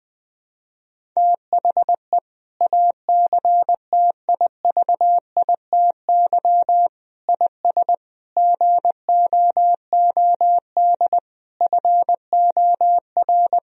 Morse code, heard as a keyed tone, about 21 wpm, 700 Hz. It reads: THE ACTIVITY IS GOOD FOR
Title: Sentences from Top 1000 Words - Review Part 4 - 20wpm